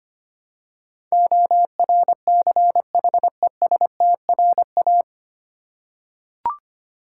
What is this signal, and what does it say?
Morse code, heard as a keyed tone, about 25 wpm, 700 Hz. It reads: ORCHESTRA
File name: practice / 25wpm / 9 Letter Words - First 3 Letters Given - 25wpm